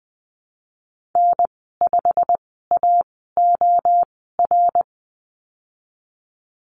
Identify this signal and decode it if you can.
Morse code, heard as a keyed tone, about 20 wpm, spaced as for 15 wpm, 700 Hz. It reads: N5AOR
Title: 140 Lesson 23g - Letter f - Call Signs 20wpm